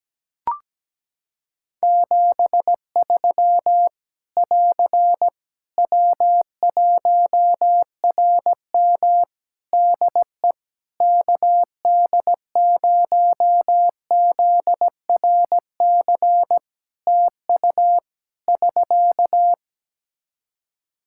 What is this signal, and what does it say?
Morse code, heard as a keyed tone, about 17 wpm, 700 Hz. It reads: 73 <AR> W1RM DE KD0ZRC TU <SK>
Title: Practice QSO - Exchange 4 - Ending - 17wpm